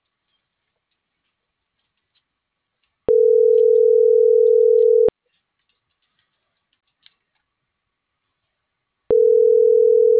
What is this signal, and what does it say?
Ringback tone (North America), 440 + 480 Hz.